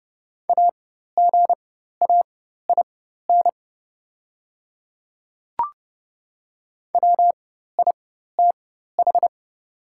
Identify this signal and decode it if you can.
Morse code, heard as a keyed tone, about 30 wpm, spaced as for 14 wpm, 700 Hz. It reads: AGAIN WITH
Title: 4x Character Spacing - Top 300 Words - 30wpm